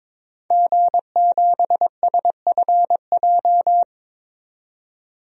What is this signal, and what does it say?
Morse code, heard as a keyed tone, about 22 wpm, 700 Hz. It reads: G7SFJ
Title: DXCC-100 Call Signs - 5 Characters - 22wpm